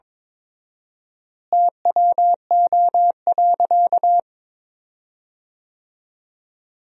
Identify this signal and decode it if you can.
Morse code, heard as a keyed tone, about 22 wpm, 700 Hz. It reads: TWO.
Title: 4x Word-Spacing - Sentences from Top 300 Words - 22wpm